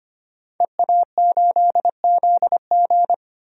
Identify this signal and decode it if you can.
Morse code, heard as a keyed tone, about 25 wpm, 690 Hz. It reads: EA8ZG
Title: DXCC-100 Call Signs - 5 Characters - 25wpm